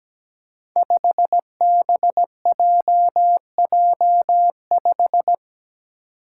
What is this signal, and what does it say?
Morse code, heard as a keyed tone, about 17 wpm, 700 Hz. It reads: 5BJJ5
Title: Five Challenging Letter-Number Combinations 17wpm